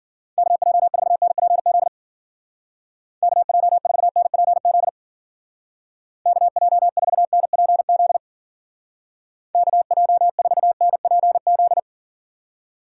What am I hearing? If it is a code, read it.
Morse code, heard as a keyed tone, about 49 wpm, 700 Hz. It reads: KJ4NPZ KJ4NPZ KJ4NPZ KJ4NPZ
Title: Speed Racing - USA 2x3 Call Signs 40wpm